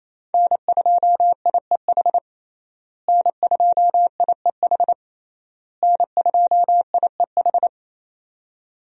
Morse code, heard as a keyed tone, about 27 wpm, 700 Hz. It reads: N2IEH N2IEH N2IEH
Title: USA Call Signs - 5 Characters - Repeated 3x - 28wpm